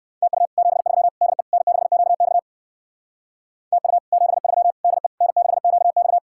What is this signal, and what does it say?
Morse code, heard as a keyed tone, about 60 wpm, 700 Hz. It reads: TU 73 DE N6QX TU 73 DE N6QX